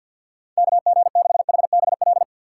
Morse code, heard as a keyed tone, about 53 wpm, 700 Hz. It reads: KG6HBL
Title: USA Call Signs - 6 Characters - 4x Character Spacing - 50wpm